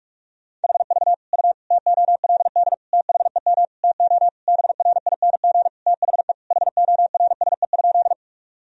Morse code, heard as a keyed tone, about 47 wpm, 700 Hz. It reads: HV U TOLD THEM TO BRING THE HORSE?
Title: Sentences from Top 500 Words and Abbreviations 45wpm